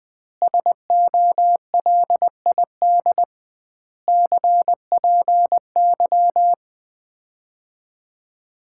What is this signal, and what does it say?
Morse code, heard as a keyed tone, about 20 wpm, 700 Hz. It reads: SOLID CPY
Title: Common QSO Phrases - 2x Word-Spacing - 20wpm